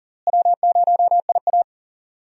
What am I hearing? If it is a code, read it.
Morse code, heard as a keyed tone, about 40 wpm, 700 Hz. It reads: W0IA
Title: USA Call Signs - 4 Characters - Repeated 3x - 40wpm